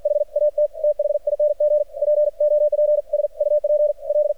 Morse code, heard as a keyed tone, about 53 wpm, 600 Hz. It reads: HAT THING POPS UP L
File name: archived / CRSnet